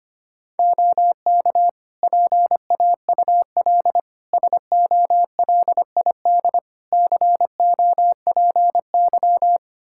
Morse code, heard as a keyed tone, about 25 wpm, 700 Hz. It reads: OK PAUL SOLID COPY
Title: K1USN SST QSO Practice 25wpm